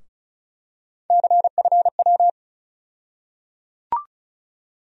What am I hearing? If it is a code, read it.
Morse code, heard as a keyed tone, about 36 wpm, 700 Hz. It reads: CFW
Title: USA Call Sign Postfixes - 3 Letters 35wpm